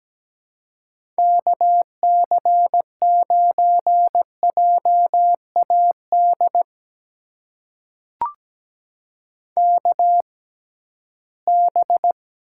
Morse code, heard as a keyed tone, about 17 wpm, spaced as for 13 wpm, 700 Hz. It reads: KC9JAD KB
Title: USA Call Signs - 6 Characters - 6x Character Spacing - 17wpm